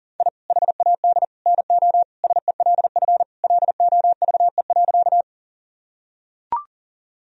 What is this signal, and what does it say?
Morse code, heard as a keyed tone, about 40 wpm, 700 Hz. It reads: I HAD NO SELF LOVE.
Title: Sentences from Top 1000 Words - Review Part 1 - 40wpm